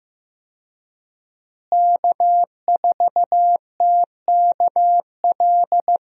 Morse code, heard as a keyed tone, about 15 wpm, 700 Hz. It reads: K4TKL
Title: USA Call Signs - 5 Characters - Repeated 2x - 15wpm